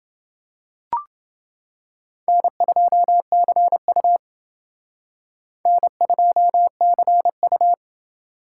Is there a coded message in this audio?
Yes: Morse code from 2.3 to 7.7 s, heard as a keyed tone, about 28 wpm, 700 Hz. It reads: N2CU N2CU